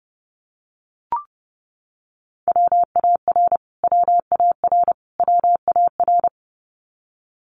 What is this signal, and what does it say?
Morse code, heard as a keyed tone, about 30 wpm, 700 Hz. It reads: WAR WAR WAR
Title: Top 500 Words Repeated 30wpm